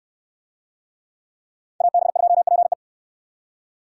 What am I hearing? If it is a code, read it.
Morse code, heard as a keyed tone, about 69 wpm, 700 Hz. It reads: AB2PE